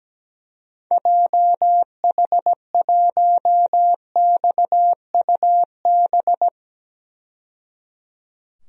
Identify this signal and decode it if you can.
Morse code, heard as a keyed tone, about 17 wpm, 700 Hz. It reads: JH1XUB
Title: DXCC-100 Call Signs - 6 Characters - 17wpm